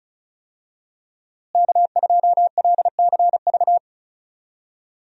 Morse code, heard as a keyed tone, about 36 wpm, 700 Hz. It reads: K2LCV
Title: USA Call Signs - 5 Characters - Repeated 2x - 35wpm